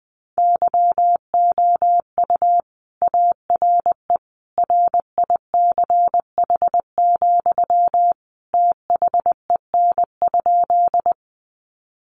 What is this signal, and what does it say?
Morse code, heard as a keyed tone, about 20 wpm, 700 Hz. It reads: YOU ARE RICH, THEN?
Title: Sentences from Top 1100 Words 20wpm